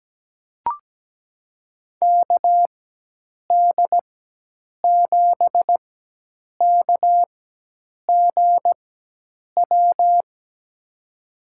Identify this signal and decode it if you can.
Morse code, heard as a keyed tone, about 17 wpm, spaced as for 8 wpm, 700 Hz. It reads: KD7KGW
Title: USA Call Signs - 6 Characters - 4x Character Spacing - 17wpm